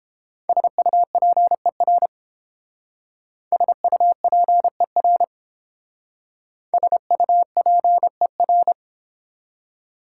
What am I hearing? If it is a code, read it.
Morse code, heard as a keyed tone, about 30 wpm, 700 Hz. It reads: SUPER SUPER SUPER